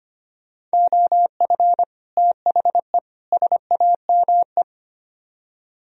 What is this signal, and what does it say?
Morse code, heard as a keyed tone, about 32 wpm, 700 Hz. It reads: OF THE SAME